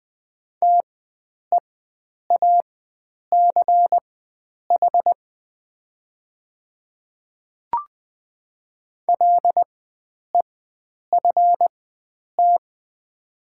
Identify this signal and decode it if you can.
Morse code, heard as a keyed tone, about 20 wpm, spaced as for 9 wpm, 700 Hz. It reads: TEACH LEFT
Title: 4x Character Spacing - Top 500 Words - 20wpm